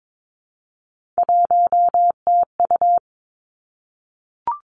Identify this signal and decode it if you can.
Morse code, heard as a keyed tone, about 22 wpm, 700 Hz. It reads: 1TU
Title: Three Challenging Letter-Number Combinations 22wpm